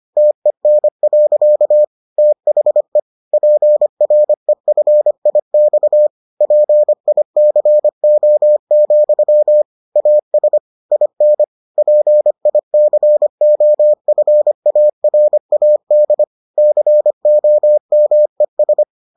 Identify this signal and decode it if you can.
Morse code, heard as a keyed tone, about 25 wpm, 600 Hz. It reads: TEN. THE PREFIX PICO, AS IN PICOFARAD COMES